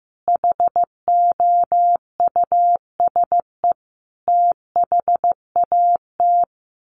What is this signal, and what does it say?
Morse code, heard as a keyed tone, about 15 wpm, 700 Hz. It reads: HOUSE THAT
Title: Sentences from Top 1200 Words 15wpm